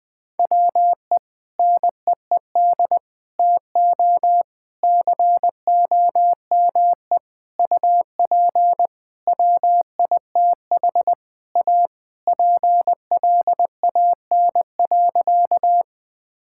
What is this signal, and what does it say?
Morse code, heard as a keyed tone, about 20 wpm, 700 Hz. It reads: WE NEED TO COME UP WITH A PLAN.
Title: Sentences with 5-Word Phrases - 20wpm